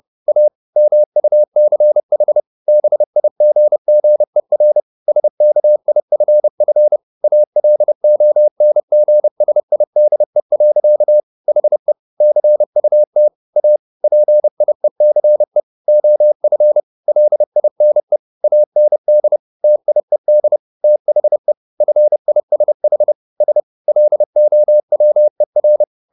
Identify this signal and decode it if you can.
Morse code, heard as a keyed tone, about 30 wpm, 600 Hz. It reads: A MUCH BIGGER SKIFF ALONGSIDE. HE CUT A PIECE OF LINE AND TIED THE FISH S LOWER